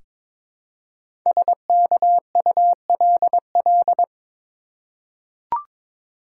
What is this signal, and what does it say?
Morse code, heard as a keyed tone, about 22 wpm, 700 Hz. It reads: SKULL
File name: 5 Letter Words - First 3 Letters Given - 22wpm